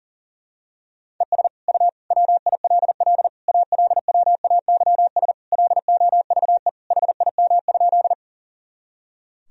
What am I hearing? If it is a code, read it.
Morse code, heard as a keyed tone, about 37 wpm, 700 Hz. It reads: ES U WILL ALWAYS LOVE HIM?
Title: Sentences from Top 400 Words and Abbreviations 40wpm